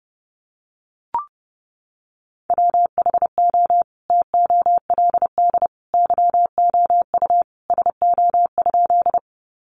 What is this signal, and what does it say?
Morse code, heard as a keyed tone, about 30 wpm, 700 Hz. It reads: WHO TOLD YOU SO?
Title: Sentences from Top 400 Words 30wpm